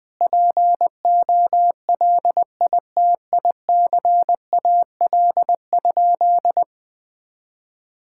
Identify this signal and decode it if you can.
Morse code, heard as a keyed tone, about 20 wpm, 700 Hz. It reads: POLITICAL?
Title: Sentences from Top 1000 Words - Review Part 5 - 20wpm